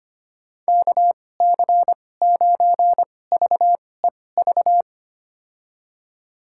Morse code, heard as a keyed tone, about 25 wpm, spaced as for 18 wpm, 700 Hz. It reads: KC9VEV
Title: USA Call Signs - 6 Characters - 2x Character Spacing - 25wpm